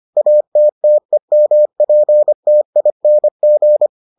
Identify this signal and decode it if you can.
Morse code, heard as a keyed tone, about 25 wpm, 600 Hz. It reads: ATTEMPTING